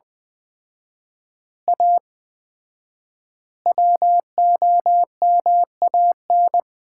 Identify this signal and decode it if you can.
Morse code, heard as a keyed tone, about 20 wpm, 700 Hz. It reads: A WOMAN
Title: 4x Word-Spacing - Sets of 5 Words - 20wpm